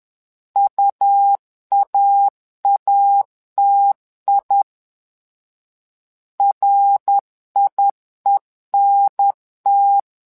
Morse code, heard as a keyed tone, about 11 wpm, 800 Hz. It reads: UAATI RIENT
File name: LEZIONI / 3 MWRO / 3 MWRO 10wpm 2148